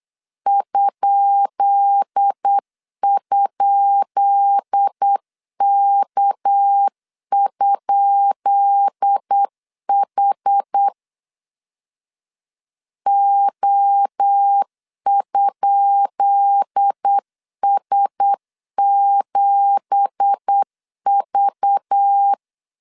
Morse code, heard as a keyed tone, about 8 wpm, 800 Hz. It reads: ??K?H O?S7V
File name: LEZIONI / 10 SIGNS / 10B INT 8wpm 3534